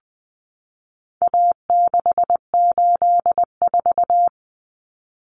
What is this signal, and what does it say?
Morse code, heard as a keyed tone, about 20 wpm, 700 Hz. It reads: A684
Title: Common Cut Numbers - 4-Digit 20wpm